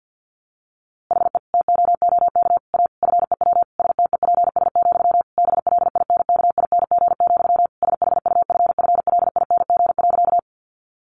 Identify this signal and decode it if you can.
Morse code, heard as a keyed tone, about 49 wpm, 700 Hz. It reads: HE TOOK A FEW STEPS, BLINKING, SHUFFLING.
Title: A Sound Of Thunder 50wpm